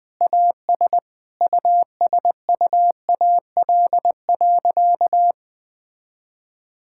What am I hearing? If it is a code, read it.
Morse code, heard as a keyed tone, about 20 wpm, 700 Hz. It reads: AS USUAL.
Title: Sentences with 5-Word Phrases - 20wpm